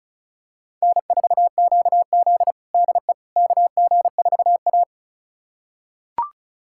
Morse code, heard as a keyed tone, about 34 wpm, 700 Hz. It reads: N4QZ DE KG4A
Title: USA Callsign DE Callsign - 4 Characters - 35wpm